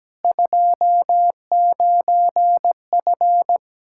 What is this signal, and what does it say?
Morse code, heard as a keyed tone, about 17 wpm, 700 Hz. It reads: 29F
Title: K1USN SST QSO Practice 17wpm